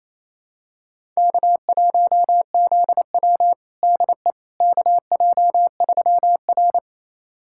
Morse code, heard as a keyed tone, about 28 wpm, 700 Hz. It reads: K1ZW DE KJ3R
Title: USA Callsign DE Callsign - 4 Characters - 28wpm